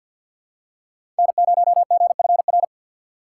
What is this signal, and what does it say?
Morse code, heard as a keyed tone, about 53 wpm, 700 Hz. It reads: N0GFR
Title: USA Call Signs - 5 Characters - 3x Character Spacing - 50wpm